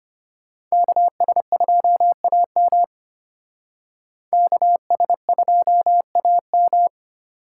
Morse code, heard as a keyed tone, about 31 wpm, 700 Hz. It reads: KS2AM KS2AM